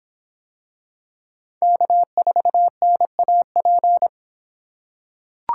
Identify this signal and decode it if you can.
Morse code, heard as a keyed tone, about 25 wpm, 700 Hz. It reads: K4NAP